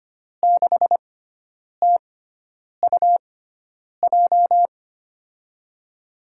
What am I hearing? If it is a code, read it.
Morse code, heard as a keyed tone, about 25 wpm, spaced as for 9 wpm, 700 Hz. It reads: 6TUJ
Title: USA Call Signs - 6 Characters - 6x Character Spacing - 25wpm